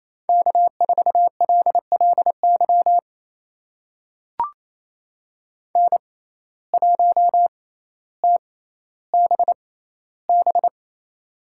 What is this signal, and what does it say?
Morse code, heard as a keyed tone, about 28 wpm, spaced as for 14 wpm, 700 Hz. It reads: K4LLY N1TBB
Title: USA Call Signs - 5 Characters - 6x Character Spacing - 28wpm